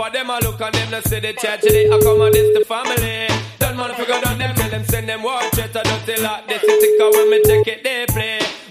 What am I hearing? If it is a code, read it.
Ringback tone (Europe, 425 Hz), with noise.